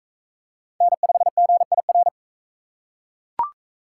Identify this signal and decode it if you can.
Morse code, heard as a keyed tone, about 40 wpm, 700 Hz. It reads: N5GIR